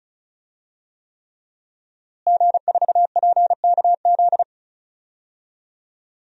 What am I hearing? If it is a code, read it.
Morse code, heard as a keyed tone, about 36 wpm, 700 Hz. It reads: G4PKZ
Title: DXCC-100 Call Signs - 5 Characters - Repeated 3x - 35wpm